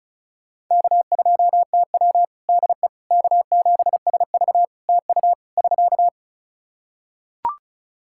Morse code, heard as a keyed tone, about 36 wpm, 700 Hz. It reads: K2TW DE K7SV TU <SK>
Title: K1USN SST QSO Practice 35wpm